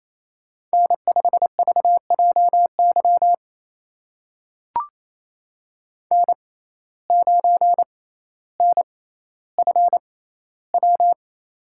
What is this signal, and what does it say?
Morse code, heard as a keyed tone, about 28 wpm, spaced as for 14 wpm, 700 Hz. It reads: N5VJY N9NFW
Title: USA Call Signs - 5 Characters - 6x Character Spacing - 28wpm